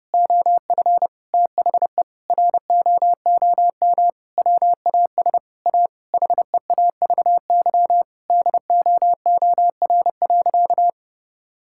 Morse code, heard as a keyed tone, about 30 wpm, 700 Hz. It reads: OF THE ROOM WAS A HEAVY DOOR.
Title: Sentences from Top 1600 Words 30wpm